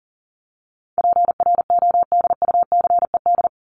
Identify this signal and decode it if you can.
Morse code, heard as a keyed tone, about 40 wpm, 700 Hz. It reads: PRODUCED
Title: ED Suffix - Top 100 Words with Base Words 40wpm